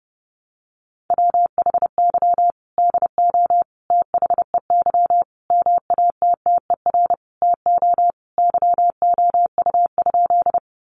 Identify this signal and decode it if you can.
Morse code, heard as a keyed tone, about 30 wpm, 700 Hz. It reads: WHY DO THEY MATTER TO YOU?
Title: Sentences from Top 700 Words 30wpm